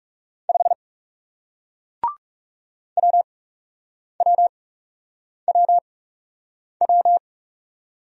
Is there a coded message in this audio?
Yes: Morse code from 0.5 to 7.2 s, heard as a keyed tone, about 41 wpm, spaced as for 8 wpm, 700 Hz. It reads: 5 WWWW